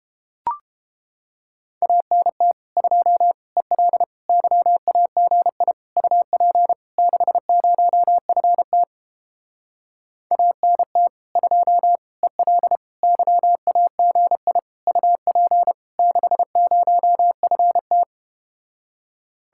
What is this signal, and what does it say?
Morse code, heard as a keyed tone, about 32 wpm, 700 Hz. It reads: ANT 2 EL YAGI UP 60FT ANT 2 EL YAGI UP 60FT